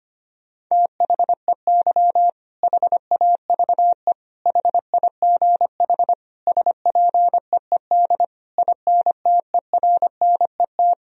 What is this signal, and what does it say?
Morse code, heard as a keyed tone, about 25 wpm, 700 Hz. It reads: THEY HAVE HIGH SPEED INTERNET
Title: Sentences from Top 1000 Words - Review Part 4 - 25wpm